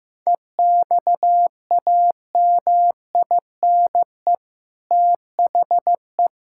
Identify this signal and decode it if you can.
Morse code, heard as a keyed tone, about 15 wpm, 700 Hz. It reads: EXAMINE THE
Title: Sentences from Top 1600 Words 15wpm